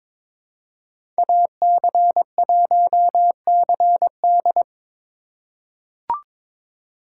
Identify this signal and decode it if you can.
Morse code, heard as a keyed tone, about 21 wpm, 700 Hz. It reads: AC1CD
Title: USA Call Signs - 5 Characters - 2x Character Spacing - 22wpm